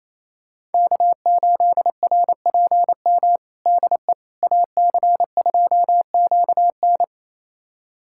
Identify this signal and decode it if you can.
Morse code, heard as a keyed tone, about 28 wpm, 700 Hz. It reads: K8RPM DE AC2QN
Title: USA Callsign DE Callsign - 5 Characters - 28wpm